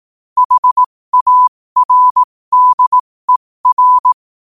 Morse code, heard as a keyed tone, about 18 wpm, spaced as for 15 wpm, 1000 Hz. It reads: HARDER
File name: qotdfiles / 15wpm